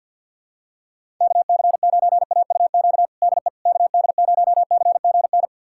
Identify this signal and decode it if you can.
Morse code, heard as a keyed tone, about 48 wpm, 700 Hz. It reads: KC9AUQ DE KD0CGN
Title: USA Callsign DE Callsign - 6 Characters - 50wpm